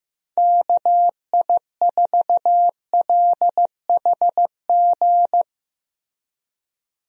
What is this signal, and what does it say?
Morse code, heard as a keyed tone, about 16 wpm, 700 Hz. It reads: KI4LHG
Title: USA Call Signs - 6 Characters - Repeated 2x - 15wpm